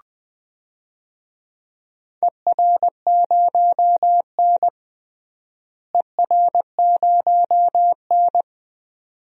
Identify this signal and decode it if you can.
Morse code, heard as a keyed tone, about 20 wpm, 700 Hz. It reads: ER0N ER0N
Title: DXCC-100 Call Signs - 4 Characters - Repeated 2x - 20wpm